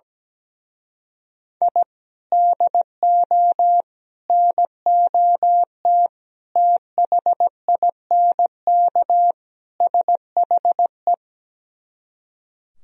Morse code, heard as a keyed tone, about 17 wpm, 700 Hz. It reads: I DO NOT THINK SHE
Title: Speed-Racing - Sets of 5 Words 17wpm